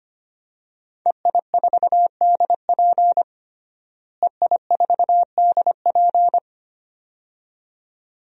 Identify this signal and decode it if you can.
Morse code, heard as a keyed tone, about 25 wpm, 700 Hz. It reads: EI4DP EI4DP
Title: DXCC-100 Call Signs - 5 Characters - Repeated 2x - 25wpm